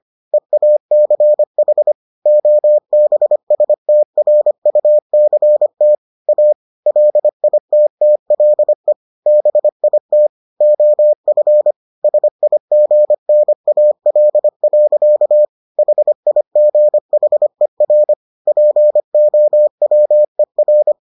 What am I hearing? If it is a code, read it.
Morse code, heard as a keyed tone, about 25 wpm, 600 Hz. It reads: EACH OBSTRUCT A LITTLE BIT OF SIGNAL. HIGHER POWER